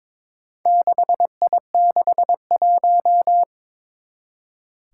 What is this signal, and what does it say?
Morse code, heard as a keyed tone, about 22 wpm, 700 Hz. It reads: 6I61
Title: Four Challenging Letter-Number Combinations 22wpm